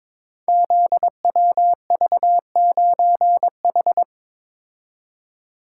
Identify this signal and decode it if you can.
Morse code, heard as a keyed tone, about 22 wpm, 700 Hz. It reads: ZWV9H
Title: Code Groups - 5 Letter:Numbers - 22wpm